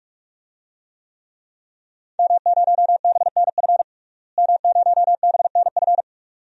Morse code, heard as a keyed tone, about 43 wpm, 700 Hz. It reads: M0BNF M0BNF
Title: DXCC-100 Call Signs - 5 Characters - Repeated 2x - 45wpm